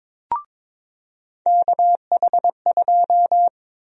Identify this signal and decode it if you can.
Morse code, heard as a keyed tone, about 22 wpm, 700 Hz. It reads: KH2
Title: USA Call Sign Prefixes - 22wpm